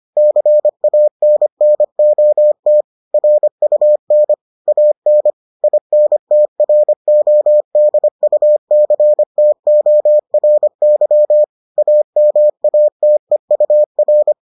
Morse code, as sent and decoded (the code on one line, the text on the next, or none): -.-. .- -. -. --- - / .-. ..- -. / .- -. / .. -. - .-. --- -.. ..- -.-. - --- .-. -.-- / .- -- .- - . ..- .-.
CANNOT RUN AN INTRODUCTORY AMATEUR